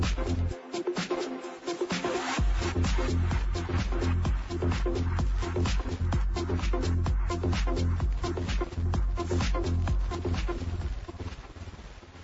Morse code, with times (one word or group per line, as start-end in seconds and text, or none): none